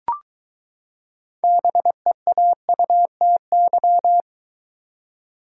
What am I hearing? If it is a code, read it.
Morse code, heard as a keyed tone, about 23 wpm, 700 Hz. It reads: BEAUTY